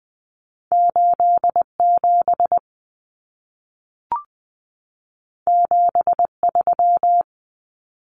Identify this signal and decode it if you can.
Morse code, heard as a keyed tone, about 20 wpm, 700 Hz. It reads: 87 73
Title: Common Cut Numbers - 2-Digit 20wpm